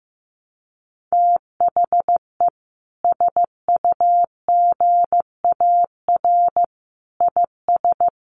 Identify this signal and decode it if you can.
Morse code, heard as a keyed tone, about 14 wpm, 700 Hz. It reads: THE SUGAR IS
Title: Sentences from Top 1200 Words 15wpm